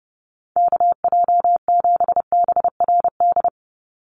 Morse code, heard as a keyed tone, about 30 wpm, 700 Hz. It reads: KJ7BRD
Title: USA Call Signs 30wpm